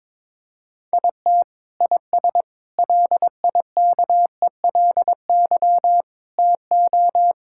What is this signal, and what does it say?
Morse code, heard as a keyed tone, about 22 wpm, 700 Hz. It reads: IT IS LIKELY TO
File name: practice / 22wpm / Sentences with 2-Word Phrases - 22wpm